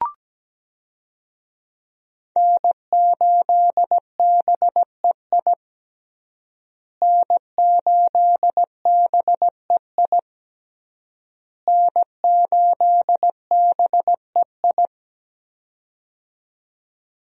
Morse code, heard as a keyed tone, about 17 wpm, 700 Hz. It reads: N8BEI N8BEI N8BEI
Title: USA Call Signs - 5 Characters - Repeated 3x - 17wpm